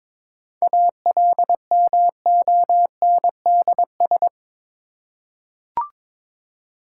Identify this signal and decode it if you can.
Morse code, heard as a keyed tone, about 22 wpm, 690 Hz. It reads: ALMONDS
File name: Difficult Words with Hints - 22wpm